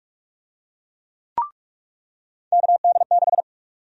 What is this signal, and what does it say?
Morse code, heard as a keyed tone, about 43 wpm, 700 Hz. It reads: KD6
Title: USA Call Sign Prefixes - 45wpm